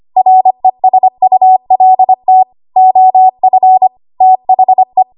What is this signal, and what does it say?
Morse code, heard as a keyed tone, about 25 wpm, 750 Hz. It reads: RESULT OF THE
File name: Archive / 25 WPM / 25WPM